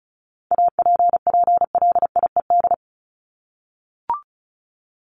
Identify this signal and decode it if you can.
Morse code, heard as a keyed tone, about 34 wpm, 700 Hz. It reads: APPLIED
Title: ED Suffix - Top 100 Words with Base Words 35wpm